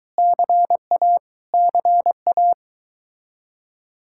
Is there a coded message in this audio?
Yes: Morse code from 0.2 to 2.5 s, heard as a keyed tone, about 23 wpm, 700 Hz. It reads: CA CA